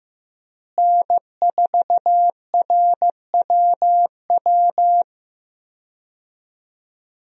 Morse code, heard as a keyed tone, about 15 wpm, 700 Hz. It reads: N4RWW
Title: USA Call Signs - 5 Characters - Repeated 2x - 15wpm